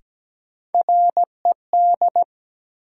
Morse code, heard as a keyed tone, about 17 wpm, 700 Hz. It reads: RED
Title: Sentences from Top 700 Words - New Word Drill - 17wpm